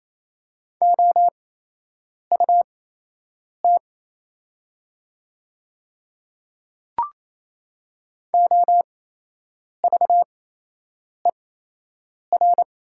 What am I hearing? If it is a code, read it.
Morse code, heard as a keyed tone, about 27 wpm, spaced as for 8 wpm, 700 Hz. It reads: OUT OVER
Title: 8x Character Spacing - Top 100 Words - 28wpm